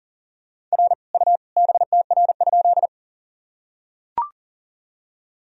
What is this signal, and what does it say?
Morse code, heard as a keyed tone, about 40 wpm, 700 Hz. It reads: R U BTR?
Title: Sentences from Top 400 Words and Abbreviations 40wpm